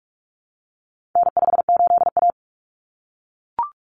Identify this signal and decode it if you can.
Morse code, heard as a keyed tone, about 44 wpm, 700 Hz. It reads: N58A
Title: Common Cut Numbers - 4-Digit 45wpm